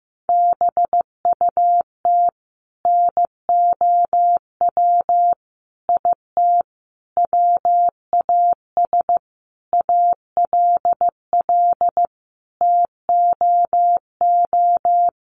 Morse code, heard as a keyed tone, about 15 wpm, 700 Hz. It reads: BUT NOW IT WAS ALL TOO
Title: Sentences from Top 700 Words 15wpm